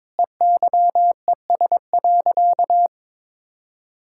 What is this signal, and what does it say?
Morse code, heard as a keyed tone, about 22 wpm, 700 Hz. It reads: EYES.
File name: practice / 22wpm / Easy Sentences with Hints - 22wpm